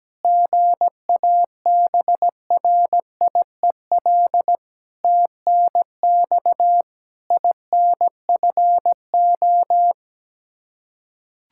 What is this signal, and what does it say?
Morse code, heard as a keyed tone, about 17 wpm, 700 Hz. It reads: GABRIEL TNX INFO
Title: Practice QSO - Exchange 3 - More Info - 17wpm